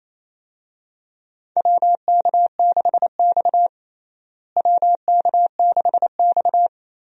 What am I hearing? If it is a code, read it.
Morse code, heard as a keyed tone, about 28 wpm, 700 Hz. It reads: WK6X WK6X
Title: USA Call Signs - 4 Characters - Repeated 2x - 28wpm